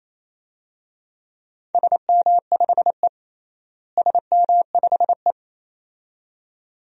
Morse code, heard as a keyed tone, about 28 wpm, 700 Hz. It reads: SM5E SM5E
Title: DXCC-100 Call Signs - 4 Characters - Repeated 2x - 28wpm